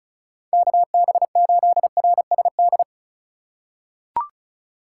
Morse code, heard as a keyed tone, about 36 wpm, 700 Hz. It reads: KB8RSD
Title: USA Call Signs - 6 Characters - 4x Character Spacing - 35wpm